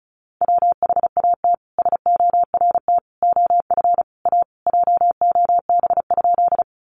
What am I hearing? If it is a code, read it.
Morse code, heard as a keyed tone, about 34 wpm, 700 Hz. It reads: WHAT SORT OF A JOB?
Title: Sentences from Top 1300 Words 35wpm